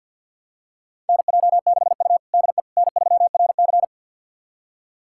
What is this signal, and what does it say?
Morse code, heard as a keyed tone, about 47 wpm, 700 Hz. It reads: NJ6U DE N3RC
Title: USA Callsign DE Callsign - 4 Characters - 50wpm